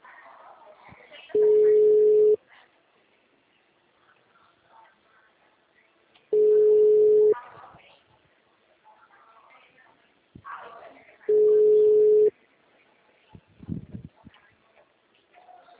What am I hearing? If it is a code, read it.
Ringback tone (Europe, 425 Hz), with speech.